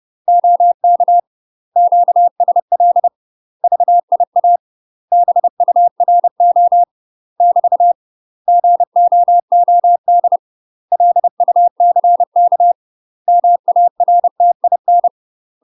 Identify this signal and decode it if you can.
Morse code, heard as a keyed tone, about 30 wpm, 700 Hz. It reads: OK QSL VIA BURO <BT> GOOD LUCK MARTIN
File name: UK MASTER QSOs TRAINING / 30WPM / UK MASTER QSOs 0016